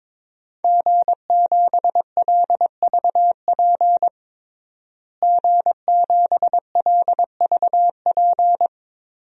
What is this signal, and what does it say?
Morse code, heard as a keyed tone, about 22 wpm, 700 Hz. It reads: G7LVP G7LVP
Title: DXCC-100 Call Signs - 5 Characters - Repeated 2x - 22wpm